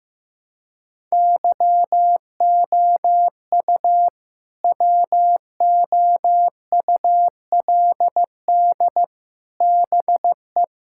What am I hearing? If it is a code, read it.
Morse code, heard as a keyed tone, about 14 wpm, 700 Hz. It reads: YOU WOULD BE
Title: Sentences from Top 1000 Words - Review Part 6 - 15wpm